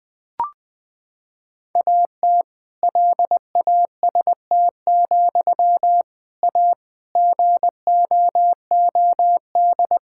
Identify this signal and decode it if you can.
Morse code, heard as a keyed tone, about 19 wpm, 700 Hz. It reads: AT LAST, A GOOD
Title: Sentences from Top 1600 Words 20wpm